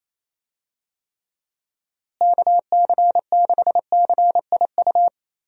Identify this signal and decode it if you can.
Morse code, heard as a keyed tone, about 28 wpm, 700 Hz. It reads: KC6CIU
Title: USA Call Signs - 6 Characters - Repeated 3x - 28wpm